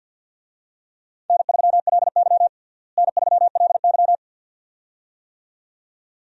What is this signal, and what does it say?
Morse code, heard as a keyed tone, about 47 wpm, 700 Hz. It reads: N3LY N3LY
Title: USA Call Signs - 4 Characters - Repeated 2x - 50wpm